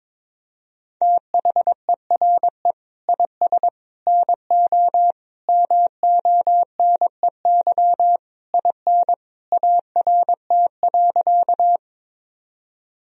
Morse code, heard as a keyed tone, about 22 wpm, 700 Hz. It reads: THERE IS NO MONEY IN ART.
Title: Sentences from Top 1000 Words - Review Part 3 - 22wpm